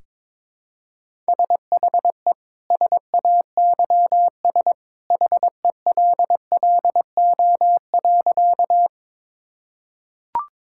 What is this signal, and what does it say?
Morse code, heard as a keyed tone, about 22 wpm, 700 Hz. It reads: SHE SAYS HELLO.
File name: practice / 22wpm / Sentences with 2-Word Phrases - 22wpm